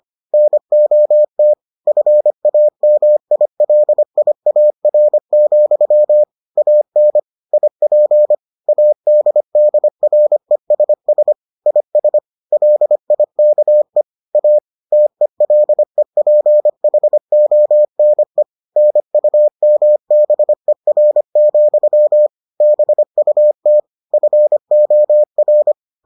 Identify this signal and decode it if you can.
Morse code, heard as a keyed tone, about 25 wpm, 600 Hz. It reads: NOT FAMILIAR, AN IP ADDRESS IS LIKE A TELEPHONE NUMBER, BUT FOR